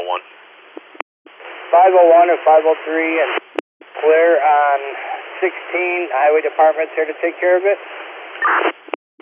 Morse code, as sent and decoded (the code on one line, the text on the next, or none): none